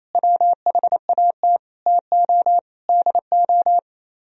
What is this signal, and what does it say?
Morse code, heard as a keyed tone, about 36 wpm, 700 Hz. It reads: WHAT TO DO